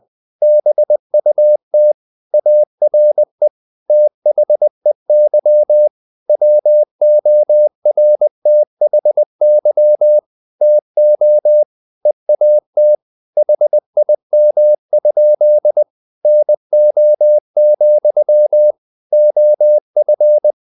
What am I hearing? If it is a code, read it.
Morse code, heard as a keyed tone, about 20 wpm, 600 Hz. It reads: BUT ARE THEY WORTHY TO EAT HIM? NO, OF
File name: the old man and the sea part 2 20wpm